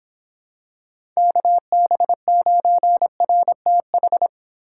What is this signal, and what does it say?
Morse code, heard as a keyed tone, about 26 wpm, 700 Hz. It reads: KB9RTH